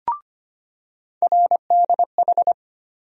Morse code, heard as a keyed tone, about 25 wpm, 700 Hz. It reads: RDH
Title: USA Call Sign Postfixes - 3 Letters 25wpm